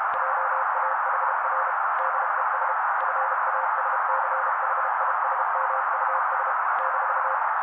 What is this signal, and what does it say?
Morse code, heard as a keyed tone, about 32 wpm, 530 Hz. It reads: JAHR DES FASCHISMUS -
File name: daily / 31wpm noise